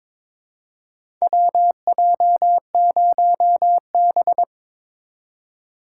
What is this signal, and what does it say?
Morse code, heard as a keyed tone, about 23 wpm, 700 Hz. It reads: WJ0B
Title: USA Call Signs - 4 Characters - 3x Character Spacing - 22wpm